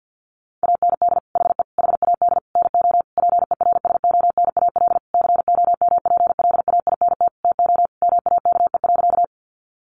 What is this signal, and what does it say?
Morse code, heard as a keyed tone, about 50 wpm, 700 Hz. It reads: AND HE HAD NO PERSONAL COMPLAINT TO MAKE.
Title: Sentences from Top 1400 Words 50wpm